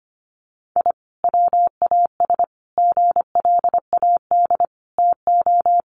Morse code, heard as a keyed tone, about 25 wpm, 700 Hz. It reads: I WAS GLAD TO
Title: Sentences from Top 1100 Words 25wpm